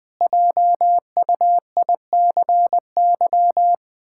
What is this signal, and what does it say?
Morse code, heard as a keyed tone, about 21 wpm, 700 Hz. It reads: JUICY
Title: Pangram Challenge v2 - 20wpm